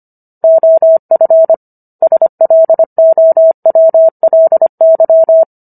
Morse code, heard as a keyed tone, about 25 wpm, 650 Hz. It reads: OF SLOWLY